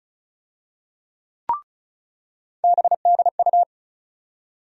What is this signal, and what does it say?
Morse code, heard as a keyed tone, about 34 wpm, 700 Hz. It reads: BDU
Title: USA Call Sign Postfixes - 3 Letters 35wpm